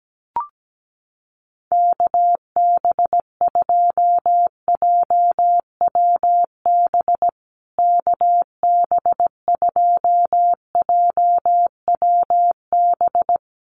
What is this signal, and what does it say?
Morse code, heard as a keyed tone, about 17 wpm, 700 Hz. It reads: KB2JWB KB2JWB
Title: USA Call Signs Repeated 3x 17wpm